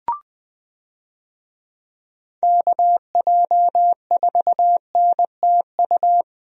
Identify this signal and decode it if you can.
Morse code, heard as a keyed tone, about 20 wpm, 700 Hz. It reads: KJ4NTU
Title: USA Call Signs - 6 Characters - Repeated 3x - 20wpm